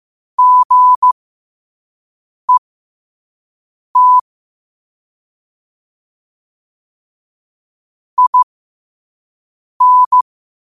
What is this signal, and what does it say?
Morse code, heard as a keyed tone, about 15 wpm, spaced as for 5 wpm, 1000 Hz. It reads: GET IN
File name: qotdfiles / 5wpm